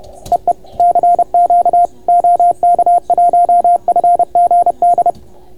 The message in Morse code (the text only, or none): EE CQ OK1FGD